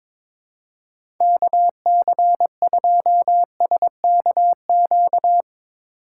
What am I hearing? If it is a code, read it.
Morse code, heard as a keyed tone, about 22 wpm, 700 Hz. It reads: KC2SKQ
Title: USA Call Signs - 6 Characters - 4x Character Spacing - 22wpm